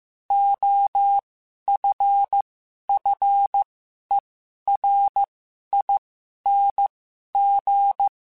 Morse code, heard as a keyed tone, about 15 wpm, spaced as for 11 wpm, 800 Hz. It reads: OFFERING
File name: Special INGs-mp3s / Special INGs-mp3s / Sngl Ing @10